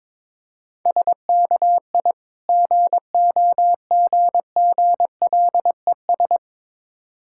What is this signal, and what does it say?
Morse code, heard as a keyed tone, about 22 wpm, 690 Hz. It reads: SKI GOGGLES
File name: Difficult Words with Hints - 22wpm